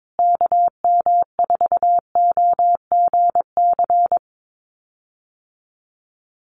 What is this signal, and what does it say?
Morse code, heard as a keyed tone, about 22 wpm, 700 Hz. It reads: KM4OGC
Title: USA Call Signs Repeated 3x 22wpm